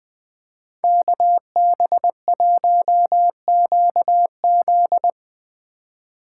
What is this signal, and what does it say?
Morse code, heard as a keyed tone, about 20 wpm, 700 Hz. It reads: KB1QZ
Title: Speed Racing - USA Call Signs 17wpm